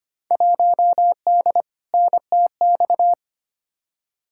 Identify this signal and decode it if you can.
Morse code, heard as a keyed tone, about 25 wpm, 700 Hz. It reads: 1D NTX
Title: Field Day Exchanges - 25wpm